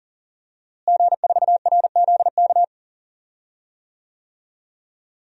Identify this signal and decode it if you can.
Morse code, heard as a keyed tone, about 40 wpm, 700 Hz. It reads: G4RZK
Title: DXCC-100 Call Signs - 5 Characters - Repeated 3x - 40wpm